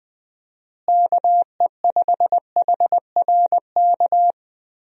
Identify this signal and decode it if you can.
Morse code, heard as a keyed tone, about 20 wpm, 700 Hz. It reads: KE5HRK
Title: USA Call Signs - 6 Characters - Repeated 2x - 20wpm